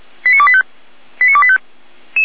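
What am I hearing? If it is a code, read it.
Five-tone selective call (ZVEI), heard as quick runs of five tones, tones of about 70 ms. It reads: 89285 89285
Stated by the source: ZVEI five-tone call 89153 received by SDR.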